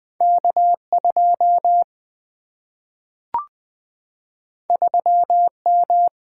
Morse code, heard as a keyed tone, about 20 wpm, 700 Hz. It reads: K2 3M
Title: Single-Letter Single-Number 20wpm